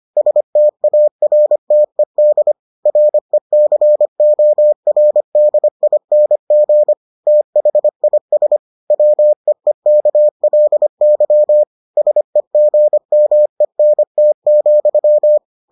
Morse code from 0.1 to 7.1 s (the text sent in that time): STARTED RECORDING